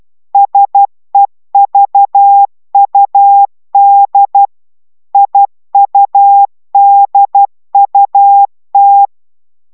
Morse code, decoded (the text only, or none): SEVUD IUDUT